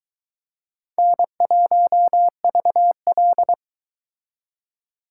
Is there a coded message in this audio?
Yes: Morse code from 1.0 to 3.5 s, heard as a keyed tone, about 23 wpm, 700 Hz. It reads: N1VL